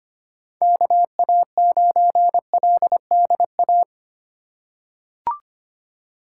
Morse code, heard as a keyed tone, about 25 wpm, 700 Hz. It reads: KA9LDA
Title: USA Call Signs - 6 Characters - 4x Character Spacing - 25wpm